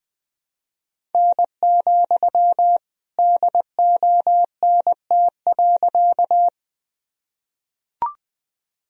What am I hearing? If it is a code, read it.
Morse code, heard as a keyed tone, about 20 wpm, 700 Hz. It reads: N, DONT.
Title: Sentences from Top 300 Words and Abbreviations 20wpm